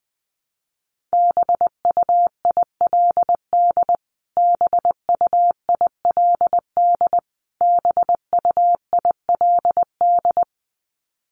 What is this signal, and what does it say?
Morse code, heard as a keyed tone, about 20 wpm, 700 Hz. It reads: BUILD BUILD BUILD
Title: Top 500 Words Repeated 20wpm